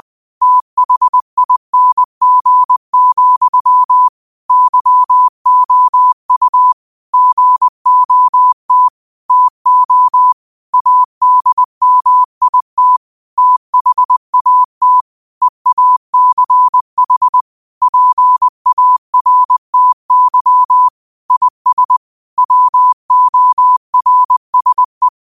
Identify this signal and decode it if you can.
Morse code, heard as a keyed tone, about 20 wpm, 1000 Hz. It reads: THING, YOU GOT TO ADMIT THAT EACH PARTY IS WORSE
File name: qotdfiles / 20wpm